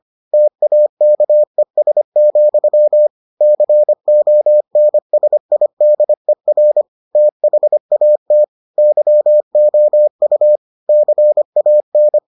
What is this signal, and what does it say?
Morse code, heard as a keyed tone, about 25 wpm, 600 Hz. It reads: TAKES, CONSIDER THAT YOU CAN